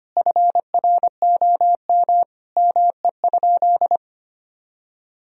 Morse code, heard as a keyed tone, about 25 wpm, 700 Hz. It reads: FROM ME?
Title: Sentences from Top 1000 Words - Review Part 4 - 25wpm